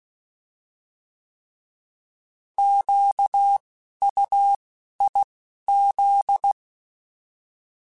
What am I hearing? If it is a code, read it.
Morse code, heard as a keyed tone, about 16 wpm, 800 Hz. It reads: Q U I Z
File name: Alphabetical / Alphabetical mp3s / Qs @10